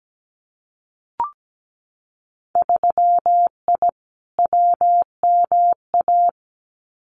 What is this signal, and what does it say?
Morse code, heard as a keyed tone, about 17 wpm, 700 Hz. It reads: 3I WMA
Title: Winter Field Day Exchanges 17wpm